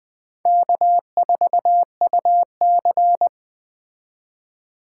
Morse code, heard as a keyed tone, about 20 wpm, 700 Hz. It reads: K4UC
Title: Speed Racing - USA 1x2 Call Signs 17wpm